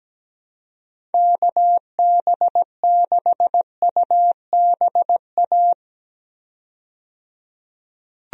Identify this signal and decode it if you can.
Morse code, heard as a keyed tone, about 17 wpm, 700 Hz. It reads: KB6UBA
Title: USA Call Signs - 6 Characters - 17wpm